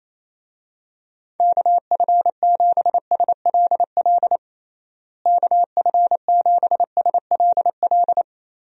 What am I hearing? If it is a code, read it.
Morse code, heard as a keyed tone, about 28 wpm, 700 Hz. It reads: KF7SLL KF7SLL
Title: USA Call Signs - 6 Characters - Repeated 2x - 28wpm